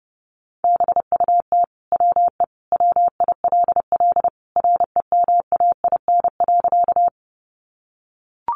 Morse code, heard as a keyed tone, about 30 wpm, 700 Hz. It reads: BUT WE WILL REMAIN.
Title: Sentences from Top 1200 Words 30wpm